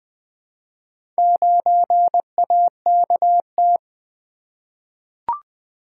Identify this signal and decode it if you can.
Morse code, heard as a keyed tone, about 20 wpm, 700 Hz. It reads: 9AKT
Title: Four Letter-Number Combinations - 20wpm